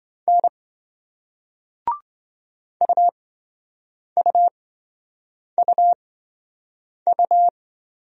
Morse code, heard as a keyed tone, about 24 wpm, spaced as for 7 wpm, 700 Hz. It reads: N UUUU
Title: Speed-Racing Single Letters 20wpm